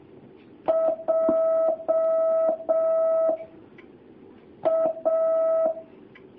Morse code, heard as a keyed tone, about 6 wpm, 650 Hz. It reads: J A
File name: jh8cbh / 2019cw